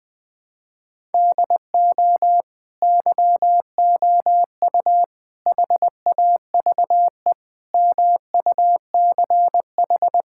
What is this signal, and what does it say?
Morse code, heard as a keyed tone, about 21 wpm, 700 Hz. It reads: DO YOU HAVE MUCH
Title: Sentences from Top 1000 Words - Review Part 6 - 20wpm